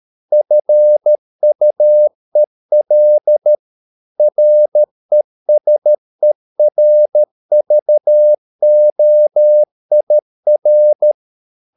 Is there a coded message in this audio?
Yes: Morse code from 0.3 to 11.1 s, heard as a keyed tone, about 13 wpm, 600 Hz. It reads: FUEL RESERVOIR